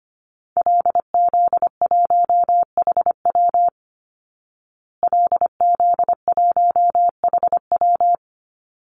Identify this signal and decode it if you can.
Morse code, heard as a keyed tone, about 25 wpm, 700 Hz. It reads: LZ1HW LZ1HW
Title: Call Signs Repeated 3x - 25wpm